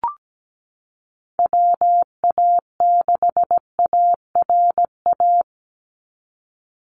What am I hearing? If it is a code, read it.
Morse code, heard as a keyed tone, about 17 wpm, 700 Hz. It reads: WA6ARA
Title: NA SOTA Chaser Callsigns 17wpm